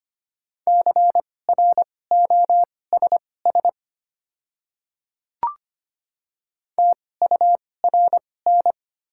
Morse code, heard as a keyed tone, about 25 wpm, spaced as for 18 wpm, 700 Hz. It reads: CROSS TURN
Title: 2x Character Spacing - Top 300 Words - 25wpm